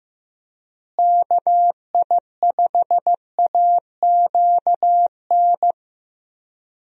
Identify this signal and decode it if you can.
Morse code, heard as a keyed tone, about 15 wpm, 700 Hz. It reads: KI5AQN
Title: USA Call Signs - 6 Characters - Repeated 2x - 15wpm